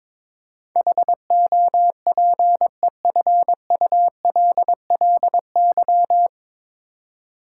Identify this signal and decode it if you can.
Morse code, heard as a keyed tone, about 22 wpm, 700 Hz. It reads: HOPEFULLY
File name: 9 Letter Words - First 3 Letters Given - 22wpm